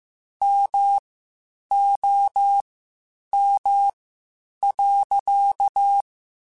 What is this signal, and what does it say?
Morse code, heard as a keyed tone, about 15 wpm, spaced as for 8 wpm, 800 Hz. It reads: MOM.
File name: (spaced) / QSOs Spaced-mp3s / WA7QOX 1 @13 sp